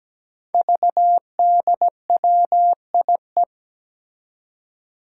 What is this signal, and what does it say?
Morse code, heard as a keyed tone, about 17 wpm, 700 Hz. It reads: VDWIE
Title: Five Challenging Letter-Number Combinations 17wpm